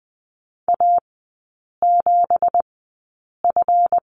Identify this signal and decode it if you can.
Morse code, heard as a keyed tone, about 20 wpm, spaced as for 8 wpm, 700 Hz. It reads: A7F
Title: 166 Lesson 27e - Number 7 - Three Characters 20wpm